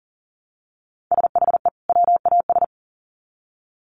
Morse code, heard as a keyed tone, about 40 wpm, 700 Hz. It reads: SHE WAS
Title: Sets of 2 Words 40wpm